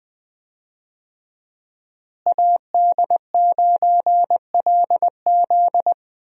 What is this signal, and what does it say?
Morse code, heard as a keyed tone, about 20 wpm, 700 Hz. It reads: AD9LZ
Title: USA Call Signs - 5 Characters - Repeated 3x - 20wpm